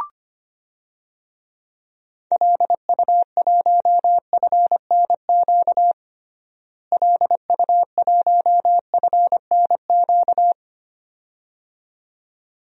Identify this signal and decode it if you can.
Morse code, heard as a keyed tone, about 25 wpm, 700 Hz. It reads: LU1FNQ LU1FNQ
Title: DXCC-100 Call Signs - 6 Characters - Repeated 2x - 25wpm